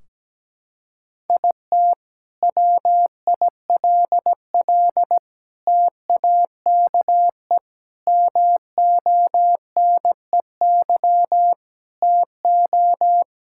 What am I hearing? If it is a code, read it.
Morse code, heard as a keyed tone, about 17 wpm, 700 Hz. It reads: IT WILL TAKE MONEY TO
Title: Sentences from Top 1600 Words 17wpm